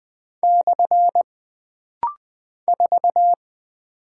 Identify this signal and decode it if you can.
Morse code, heard as a keyed tone, about 20 wpm, 700 Hz. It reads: / 4
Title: Single Character (Letter:Number:Punctuation:Prosign) - Rapid-Fire - 20wpm